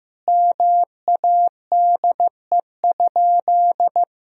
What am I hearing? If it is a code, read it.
Morse code, heard as a keyed tone, about 15 wpm, 700 Hz. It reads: MADE?
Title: Sentences from Top 1000 Words - Review Part 4 - 15wpm